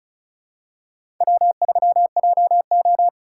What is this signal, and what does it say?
Morse code, heard as a keyed tone, about 34 wpm, 700 Hz. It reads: W3JO
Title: USA Call Signs - 4 Characters - 2x Character Spacing - 35wpm